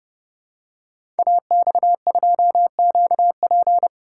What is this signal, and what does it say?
Morse code, heard as a keyed tone, about 30 wpm, 700 Hz. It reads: AX2QP
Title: Code Groups - 5 Letter:Numbers - 30wpm